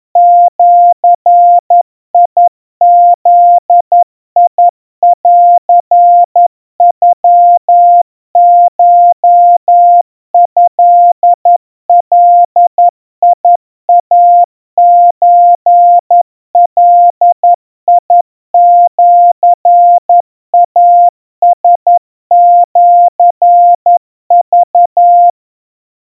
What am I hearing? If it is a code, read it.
Morse code (Japanese Wabun code), heard as a keyed tone, about 11 wpm, 700 Hz. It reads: ジブンノコトガイソガシイラシク
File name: kawa / Morse sample / kamyu 40 cpm 700 Hz Crispy